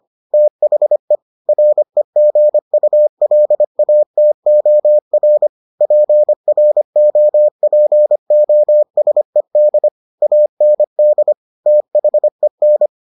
Morse code, heard as a keyed tone, about 25 wpm, 600 Hz. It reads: THE REGULATOR PROPOSED AND THEN